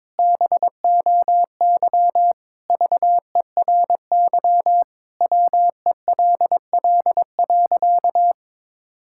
Morse code, heard as a keyed tone, about 22 wpm, 700 Hz. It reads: BOY VERY WELL.
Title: Sentences from Top 1000 Words - Review Part 5 - 22wpm